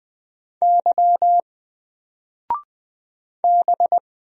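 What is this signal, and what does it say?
Morse code, heard as a keyed tone, about 20 wpm, 700 Hz. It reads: Y B
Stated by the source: Single letter and number Morse drill at 20 wpm, from N4LCD.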